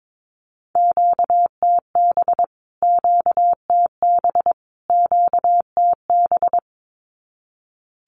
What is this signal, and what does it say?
Morse code, heard as a keyed tone, about 22 wpm, 700 Hz. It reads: QTB QTB QTB
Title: Q-Codes Repeated 3x 22wpm